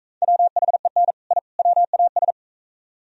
Morse code, heard as a keyed tone, about 45 wpm, 700 Hz. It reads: WHEN I WAS